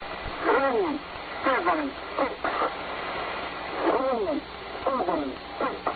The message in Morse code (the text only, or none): none